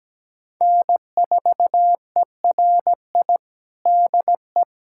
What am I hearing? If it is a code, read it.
Morse code, heard as a keyed tone, about 17 wpm, 700 Hz. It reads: N4ERI DE
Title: USA Callsign DE Callsign - 5 Characters - 17wpm